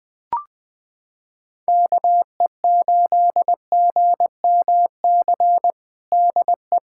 Morse code, heard as a keyed tone, about 20 wpm, 700 Hz. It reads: KE8GMC DE
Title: USA Callsign DE Callsign - 6 Characters - 20wpm